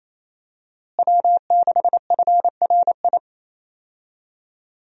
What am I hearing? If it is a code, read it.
Morse code, heard as a keyed tone, about 27 wpm, 700 Hz. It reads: W6FRI
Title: USA Call Signs - 5 Characters - Repeated 3x - 28wpm